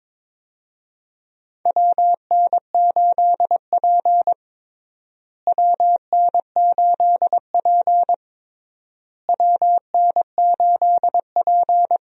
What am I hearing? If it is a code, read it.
Morse code, heard as a keyed tone, about 22 wpm, 700 Hz. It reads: WN8P WN8P WN8P
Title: USA Call Signs - 4 Characters - Repeated 3x - 22wpm